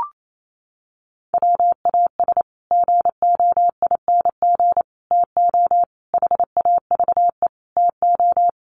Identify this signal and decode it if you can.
Morse code, heard as a keyed tone, about 28 wpm, 700 Hz. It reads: WAS GOING TO HAVE TO
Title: Sets of 5 Words Encore 28wpm